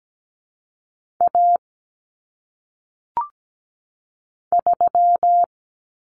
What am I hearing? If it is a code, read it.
Morse code, heard as a keyed tone, about 17 wpm, 700 Hz. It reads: A 3
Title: Common Cut Numbers - 1-Digit 17wpm